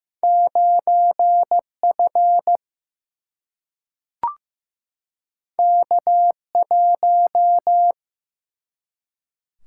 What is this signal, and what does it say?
Morse code, heard as a keyed tone, about 16 wpm, 700 Hz. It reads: 9 F K 1
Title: Single-Letter Single-Number 15wpm